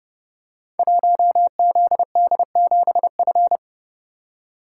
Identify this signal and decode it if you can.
Morse code, heard as a keyed tone, about 30 wpm, 700 Hz. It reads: 1ZD7F
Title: Code Groups - 5 Letter:Numbers - 30wpm